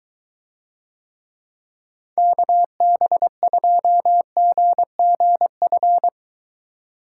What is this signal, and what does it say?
Morse code, heard as a keyed tone, about 22 wpm, 700 Hz. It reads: KB2GGF